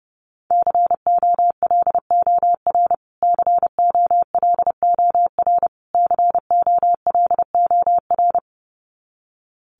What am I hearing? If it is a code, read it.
Morse code, heard as a keyed tone, about 30 wpm, 700 Hz. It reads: COLOR COLOR COLOR
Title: Top 500 Words Repeated 30wpm